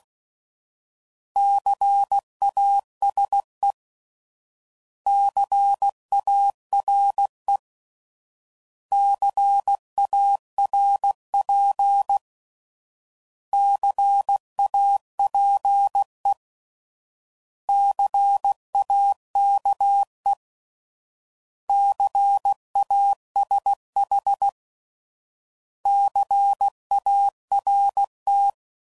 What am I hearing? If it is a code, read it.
Morse code, heard as a keyed tone, about 16 wpm, 800 Hz. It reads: CASE CARE CARP CAPE CAKE CASH CART
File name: Prefixes / By First Letter / Cs